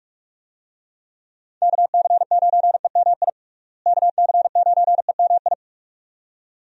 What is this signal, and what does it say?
Morse code, heard as a keyed tone, about 46 wpm, 700 Hz. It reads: KC9EMI KC9EMI
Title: USA Call Signs - 6 Characters - Repeated 2x - 45wpm